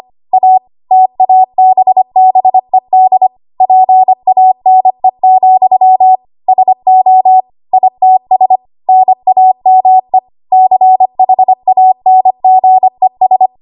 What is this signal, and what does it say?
Morse code, heard as a keyed tone, about 25 wpm, 750 Hz. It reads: A TABBED PANE, SO ITS NAME CHANGES
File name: Archive / 25 WPM / 25WPM